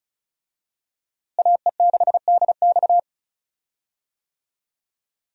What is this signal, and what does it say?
Morse code, heard as a keyed tone, about 36 wpm, 700 Hz. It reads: AE6DX
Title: USA Call Signs - 5 Characters - Repeated 3x - 35wpm